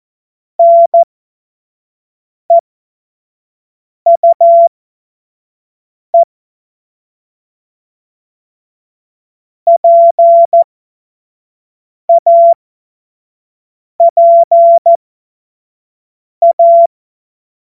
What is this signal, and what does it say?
Morse code, heard as a keyed tone, about 14 wpm, spaced as for 5 wpm, 680 Hz. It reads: NEUE PAPA
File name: Morsekurs / Witze / Witz 06. September 2022 25 F70